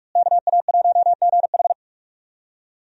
Morse code, heard as a keyed tone, about 44 wpm, 700 Hz. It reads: KA1GH
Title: USA Call Signs - 5 Characters - Repeated 3x - 45wpm